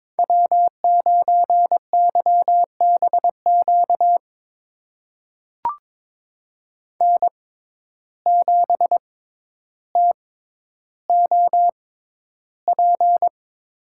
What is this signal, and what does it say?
Morse code, heard as a keyed tone, about 22 wpm, spaced as for 11 wpm, 700 Hz. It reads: W9YBQ N7TOP
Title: USA Call Signs - 5 Characters - 6x Character Spacing - 22wpm